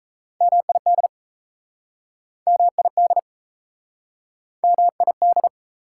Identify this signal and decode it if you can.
Morse code, heard as a keyed tone, about 38 wpm, 690 Hz. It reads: MID MID MID